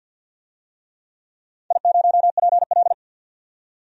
Morse code, heard as a keyed tone, about 53 wpm, 700 Hz. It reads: I0PL
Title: DXCC-100 Call Signs - 4 Characters - 50wpm